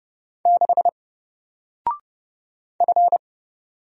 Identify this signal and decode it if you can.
Morse code, heard as a keyed tone, about 30 wpm, 700 Hz. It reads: 6 F
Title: Single Letter-Number 30wpm